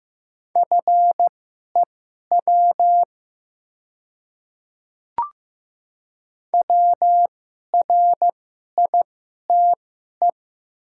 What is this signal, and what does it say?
Morse code, heard as a keyed tone, about 15 wpm, spaced as for 11 wpm, 700 Hz. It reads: FEW WRITE
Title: 2x Character Spacing - Top 300 Words - 15wpm